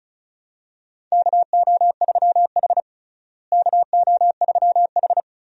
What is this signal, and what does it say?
Morse code, heard as a keyed tone, about 36 wpm, 700 Hz. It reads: KO3H KO3H
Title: USA Call Signs - 4 Characters - Repeated 3x - 35wpm